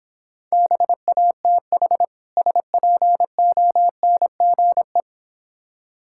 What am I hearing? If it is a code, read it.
Morse code, heard as a keyed tone, about 26 wpm, 700 Hz. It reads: BATH SPONGE